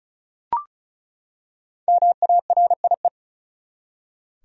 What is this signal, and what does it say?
Morse code, heard as a keyed tone, about 36 wpm, 700 Hz. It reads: MARIE
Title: USA Common Names - Max 5 Letters - 35wpm